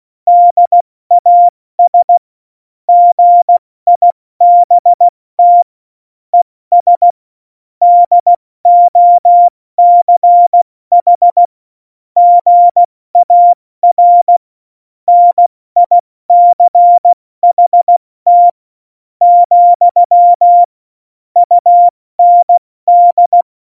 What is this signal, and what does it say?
Morse code, heard as a keyed tone, about 16 wpm, 700 Hz. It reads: DAS GIBT ES DOCH GAR NICHT , UND
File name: Morsekurs / Witze / Witz 02. Oktober 2018, 70 F80